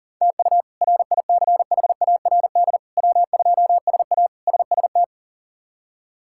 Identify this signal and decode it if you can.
Morse code, heard as a keyed tone, about 40 wpm, 700 Hz. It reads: TU RICHARD W2SA SST
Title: K1USN SST QSO Practice - 40wpm